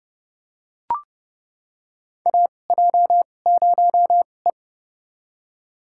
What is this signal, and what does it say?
Morse code, heard as a keyed tone, about 32 wpm, spaced as for 22 wpm, 700 Hz. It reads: AJ0E
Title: USA Call Signs - 2x Character Spacing - 30wpm